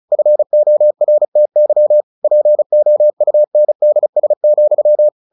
Morse code, heard as a keyed tone, about 35 wpm, 600 Hz. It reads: FORTY POUNDS,
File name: the old man and the sea 2 35wpm